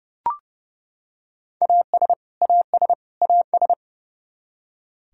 Morse code, heard as a keyed tone, about 32 wpm, 700 Hz. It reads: AS AS AS
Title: Top 100 Words Repeated 30wpm